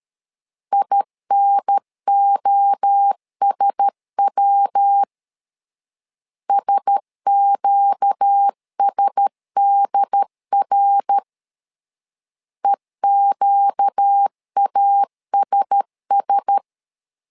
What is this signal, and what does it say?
Morse code, heard as a keyed tone, about 13 wpm, 800 Hz. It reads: INOSW SQSDR EQASS
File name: LEZIONI / 4 Q1S 12wpm 1802